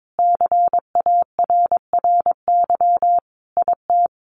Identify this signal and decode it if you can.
Morse code, heard as a keyed tone, about 22 wpm, 700 Hz. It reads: CARRY IT
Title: Sentences from Top 1400 Words 22wpm